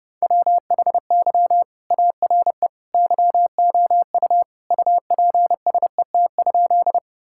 Morse code, heard as a keyed tone, about 30 wpm, 700 Hz. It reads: WHY ARE YOU UPSET?
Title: Sentences from Top 1000 Words - Review Part 5 - 30wpm